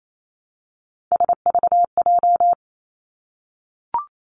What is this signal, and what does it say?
Morse code, heard as a keyed tone, about 28 wpm, 700 Hz. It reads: SVJ
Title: Three Challenging Letter-Number Combinations 28wpm